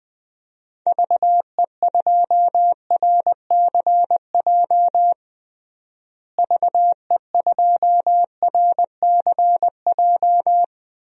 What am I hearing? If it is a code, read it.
Morse code, heard as a keyed tone, about 20 wpm, 700 Hz. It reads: VE2RCJ VE2RCJ
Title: DXCC-100 Call Signs - 6 Characters - Repeated 2x - 20wpm